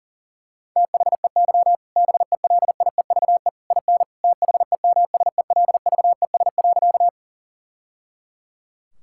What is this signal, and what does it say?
Morse code, heard as a keyed tone, about 40 wpm, 700 Hz. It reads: THEY BELIEVE IN THEMSELVES.
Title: Sentences from Top 1000 Words - Review Part 5 - 40wpm